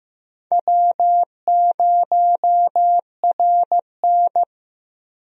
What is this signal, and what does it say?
Morse code, heard as a keyed tone, about 15 wpm, 700 Hz. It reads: W0RN
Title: USA Call Signs - 4 Characters - Repeated 3x - 15wpm